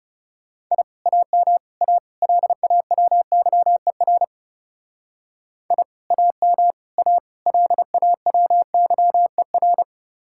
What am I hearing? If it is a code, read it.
Morse code, heard as a keyed tone, about 37 wpm, 700 Hz. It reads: I AM A LAWYER I AM A LAWYER
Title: Speed-Racing Common QSO Phrases - 30wpm